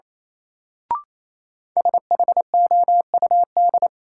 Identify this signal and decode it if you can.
Morse code, heard as a keyed tone, about 28 wpm, 700 Hz. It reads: SHOUD
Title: Sentences from Top 1000 Words - Review Part 4 - New Word Drill - 28wpm